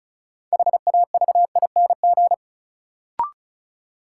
Morse code, heard as a keyed tone, about 36 wpm, 700 Hz. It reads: HAVING
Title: Sets of 5 Words Encore - New Word Drill - 35wpm